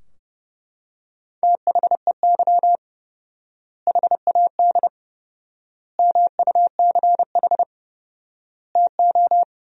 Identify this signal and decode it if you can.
Morse code, heard as a keyed tone, about 30 wpm, 700 Hz. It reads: THEY HAD MUCH TO
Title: 4x Word-Spacing - Sentences from Top 400 Words - 30wpm